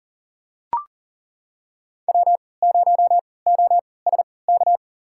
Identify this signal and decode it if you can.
Morse code, heard as a keyed tone, about 40 wpm, spaced as for 23 wpm, 700 Hz. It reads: W0OSK
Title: USA Call Signs - 5 Characters - 3x Character Spacing - 40wpm